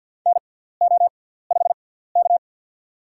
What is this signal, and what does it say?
Morse code, heard as a keyed tone, about 48 wpm, spaced as for 17 wpm, 690 Hz. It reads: NO5K
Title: USA Call Signs - 4 Characters - 6x Character Spacing - 50wpm